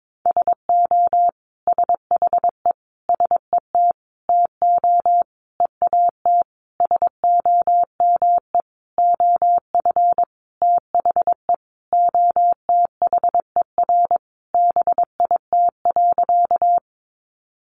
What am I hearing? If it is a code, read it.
Morse code, heard as a keyed tone, about 22 wpm, 700 Hz. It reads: SO SHE SET TO EAT SOME OF THE OTHER BIT.
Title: Sentences from Top 900 Words 22wpm